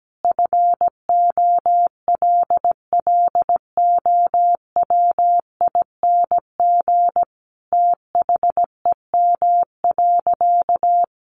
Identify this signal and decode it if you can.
Morse code, heard as a keyed tone, about 17 wpm, 700 Hz. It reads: FOLLOWING THEM.
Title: Sentences from Top 700 Words 17wpm